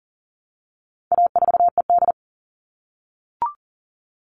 Morse code, heard as a keyed tone, about 40 wpm, 700 Hz. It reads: A4ED
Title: All Cut Numbers - 4-Digit 40wpm